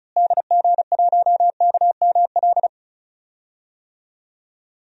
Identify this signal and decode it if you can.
Morse code, heard as a keyed tone, about 36 wpm, 700 Hz. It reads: DG1KML
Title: DXCC-100 Call Signs - 6 Characters - Repeated 3x - 35wpm